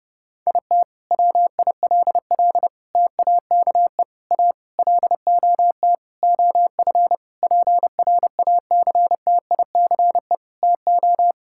Morse code, heard as a keyed tone, about 30 wpm, 700 Hz. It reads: IT WILL TAKE A LOT OF PRACTICE TO
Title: Sentences from Top 1000 Words - Review Part 6 - 30wpm